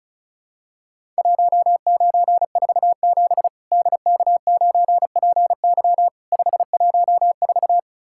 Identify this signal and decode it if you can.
Morse code, heard as a keyed tone, about 36 wpm, 700 Hz. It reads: 1947 DK9PY 514
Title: Worked All Europe DX Contest - QTC Exchange - 35wpm